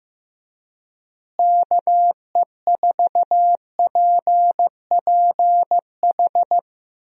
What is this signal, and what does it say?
Morse code, heard as a keyed tone, about 15 wpm, 700 Hz. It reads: KE4PPH
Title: USA Call Signs - 6 Characters - Repeated 3x - 15wpm